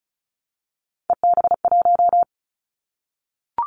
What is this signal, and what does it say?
Morse code, heard as a keyed tone, about 34 wpm, 700 Hz. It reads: EB1
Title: Callsign Prefixes for Top 25 DXCC - Repeated 3x - 35wpm